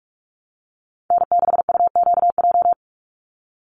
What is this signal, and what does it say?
Morse code, heard as a keyed tone, about 44 wpm, 700 Hz. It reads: N6UQJ
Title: USA Call Signs 45wpm